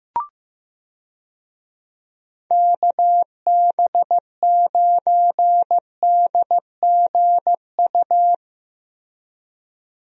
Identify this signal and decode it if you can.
Morse code, heard as a keyed tone, about 16 wpm, 700 Hz. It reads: KB9DGU
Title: USA Call Signs - 6 Characters - Repeated 3x - 15wpm